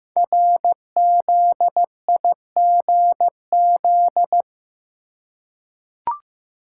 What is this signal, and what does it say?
Morse code, heard as a keyed tone, about 15 wpm, 700 Hz. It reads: RZIGZ
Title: Code Groups - 5 Letters - 15wpm